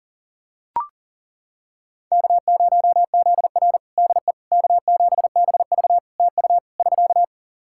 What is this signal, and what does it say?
Morse code, heard as a keyed tone, about 40 wpm, 700 Hz. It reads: K0ZR DE K7BV TU <SK>
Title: Practice QSO - Exchange 4 - Ending - 40wpm